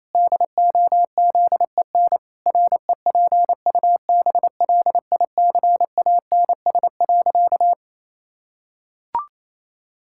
Morse code, heard as a keyed tone, about 28 wpm, 700 Hz. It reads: DOZEN REPUBLICANS.
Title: Pangram Challenge v2 - 28wpm